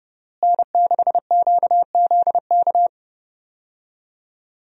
Morse code, heard as a keyed tone, about 30 wpm, 700 Hz. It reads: N6QZK
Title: USA Call Signs - 5 Characters - Repeated 3x - 30wpm